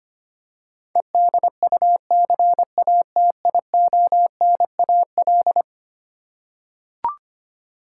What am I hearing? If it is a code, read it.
Morse code, heard as a keyed tone, about 25 wpm, 700 Hz. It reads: EDUCATIONAL
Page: morse-code-ninja-practice-25wpm